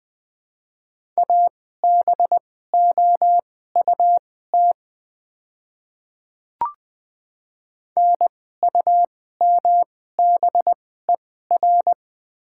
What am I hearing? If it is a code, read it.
Morse code, heard as a keyed tone, about 20 wpm, spaced as for 14 wpm, 700 Hz. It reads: ABOUT NUMBER
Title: 2x Character Spacing - Top 100 Words - 20wpm